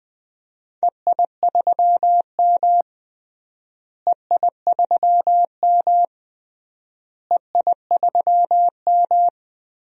Morse code, heard as a keyed tone, about 21 wpm, 700 Hz. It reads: EI3M EI3M EI3M
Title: DXCC-100 Call Signs - 4 Characters - Repeated 3x - 20wpm